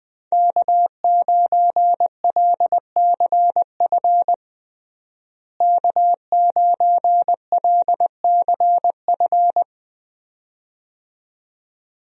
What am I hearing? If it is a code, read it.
Morse code, heard as a keyed tone, about 20 wpm, 700 Hz. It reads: K9LCF K9LCF
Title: USA Call Signs - 5 Characters - Repeated 2x - 20wpm